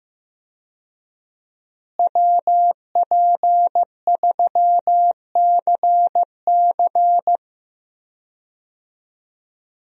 Morse code, heard as a keyed tone, about 15 wpm, 700 Hz. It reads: WP3CC
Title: USA Call Signs - 5 Characters - Repeated 2x - 15wpm